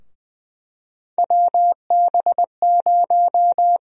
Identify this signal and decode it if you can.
Morse code, heard as a keyed tone, about 20 wpm, 700 Hz. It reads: WB0
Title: USA Call Sign Prefixes - Repeated 2x - 20wpm